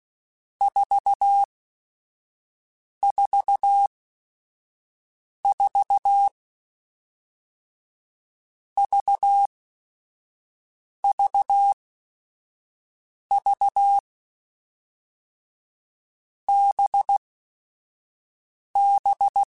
Morse code, heard as a keyed tone, about 16 wpm, spaced as for 5 wpm, 800 Hz. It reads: 444 VVV BB
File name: Groups@5 / Groups@5 mp3s / VB46@5